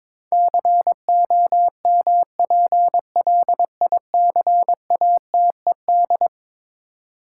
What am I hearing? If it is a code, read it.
Morse code, heard as a keyed tone, about 22 wpm, 700 Hz. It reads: COMPLICATED
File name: practice / 22wpm / Eleven Letter Words - 22wpm